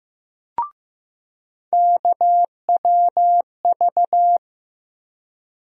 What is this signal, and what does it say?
Morse code, heard as a keyed tone, about 15 wpm, 700 Hz. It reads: KWV
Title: USA Call Sign Postfixes - 3 Letters 15wpm